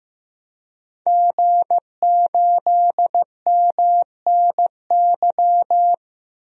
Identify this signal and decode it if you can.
Morse code, heard as a keyed tone, about 15 wpm, 700 Hz. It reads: G8MNY
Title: DXCC-100 Call Signs - 5 Characters - 15wpm